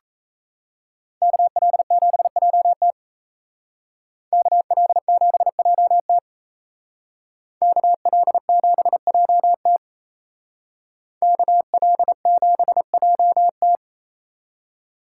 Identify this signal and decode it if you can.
Morse code, heard as a keyed tone, about 36 wpm, 700 Hz. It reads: KL7JT KL7JT KL7JT KL7JT